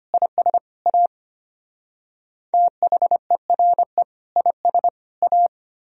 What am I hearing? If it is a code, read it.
Morse code, heard as a keyed tone, about 30 wpm, 700 Hz. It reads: IS A THERE IS A
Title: Speed-Racing - Sets of 3 Words 25wpm